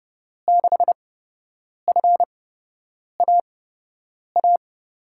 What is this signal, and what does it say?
Morse code, heard as a keyed tone, about 32 wpm, spaced as for 8 wpm, 700 Hz. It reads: 6FAA
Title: USA Call Signs - 8x Character Spacing - 30wpm